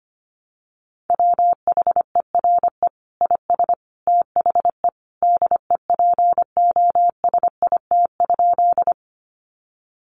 Morse code, heard as a keyed tone, about 25 wpm, 700 Hz. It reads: WHERE IS THE DEPOSIT?
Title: Sentences from Top 1300 Words 25wpm